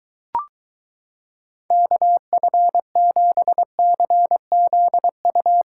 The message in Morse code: KF7CZU

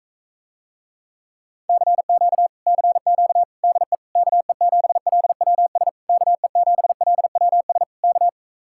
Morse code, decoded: CQ CQ DE KE7LWS KE7LWS K

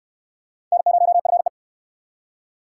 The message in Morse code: N0FE

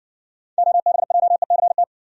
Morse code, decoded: OBJECT

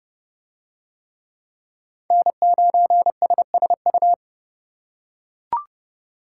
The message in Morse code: N9SSU

